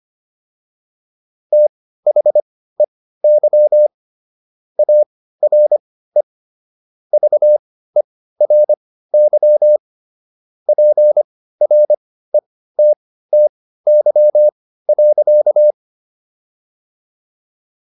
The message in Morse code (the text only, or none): THEY ARE VERY PRETTY.